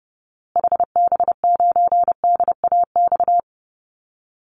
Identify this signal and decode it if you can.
Morse code, heard as a keyed tone, about 30 wpm, 700 Hz. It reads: HB9DAX
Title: Call Signs Repeated 2x - 30wpm